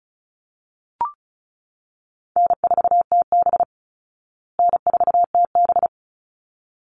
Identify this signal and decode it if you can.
Morse code, heard as a keyed tone, about 34 wpm, 700 Hz. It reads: N4TB N4TB
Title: Easy Call Signs Repeated 2x - 35wpm